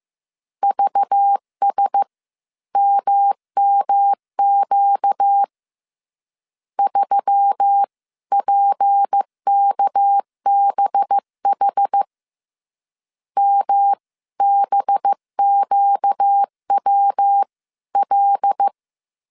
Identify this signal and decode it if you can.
Morse code, heard as a keyed tone, about 15 wpm, spaced as for 13 wpm, 800 Hz. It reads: VSMMQ 3PKBH MBQWL
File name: CW / DET / 1 6 1500C S B 14wpm 2958